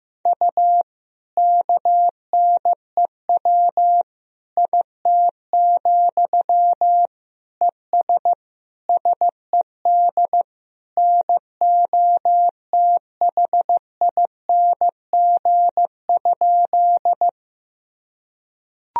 Morse code, heard as a keyed tone, about 15 wpm, 700 Hz. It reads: U KNEW IT, ES SED NOTHING?
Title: Sentences from Top 500 Words and Abbreviations 15wpm